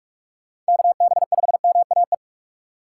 Morse code, heard as a keyed tone, about 65 wpm, spaced as for 55 wpm, 700 Hz. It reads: KB5MAE
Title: Speed Racing - USA 2x3 Call Signs 45wpm